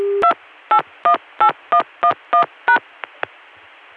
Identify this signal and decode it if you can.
DTMF, heard as telephone touch tones, tones of about 90 ms. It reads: 2818111#